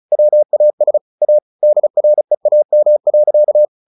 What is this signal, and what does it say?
Morse code, heard as a keyed tone, about 35 wpm, 600 Hz. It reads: WAS A DREAM.
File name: the old man and the sea 2 35wpm